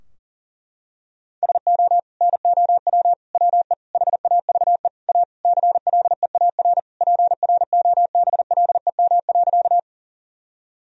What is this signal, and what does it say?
Morse code, heard as a keyed tone, about 40 wpm, 700 Hz. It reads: SO NOW WE HAVE A CLEAR PROBLEM.
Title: Sentences from Top 1000 Words - Review Part 2 - 40wpm